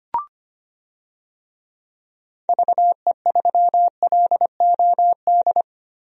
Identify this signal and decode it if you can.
Morse code, heard as a keyed tone, about 25 wpm, 700 Hz. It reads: VE3LOD
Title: DXCC-100 Call Signs - 6 Characters - Repeated 2x - 25wpm